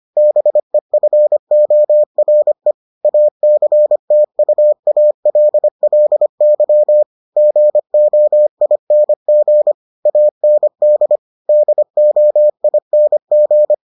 Morse code, heard as a keyed tone, about 25 wpm, 600 Hz. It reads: BEFORE ACTUALLY GOING AND DOING